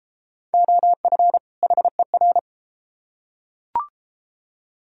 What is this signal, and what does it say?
Morse code, heard as a keyed tone, about 30 wpm, 700 Hz. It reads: OF HER